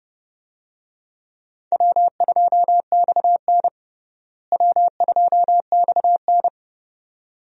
Morse code, heard as a keyed tone, about 29 wpm, 700 Hz. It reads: W2XN W2XN
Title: USA Call Signs - 4 Characters - Repeated 2x - 30wpm